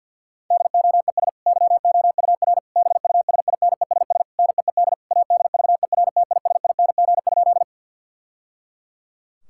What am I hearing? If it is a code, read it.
Morse code, heard as a keyed tone, about 52 wpm, 700 Hz. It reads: DOES YOUR BUSINESS NEED ADVERTISING?
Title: Sentences from Top 1000 Words - Review Part 6 - 50wpm